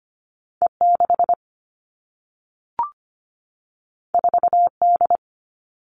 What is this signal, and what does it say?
Morse code, heard as a keyed tone, about 25 wpm, 700 Hz. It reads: E6 4D
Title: All Cut Numbers - 2-Digit 25wpm